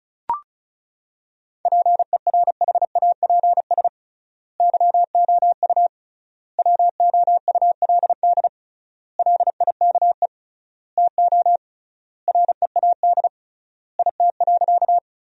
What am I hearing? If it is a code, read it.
Morse code, heard as a keyed tone, about 36 wpm, 700 Hz. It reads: PERHAPS YOU WOULD LIKE TO READ IT.
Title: 3x Word-Spacing - Sentences from Top 500 Words - 35wpm